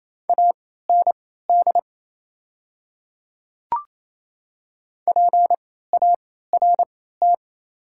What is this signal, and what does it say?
Morse code, heard as a keyed tone, about 28 wpm, spaced as for 16 wpm, 700 Hz. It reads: AND PART
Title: 3x Character Spacing - Top 300 Words - 28wpm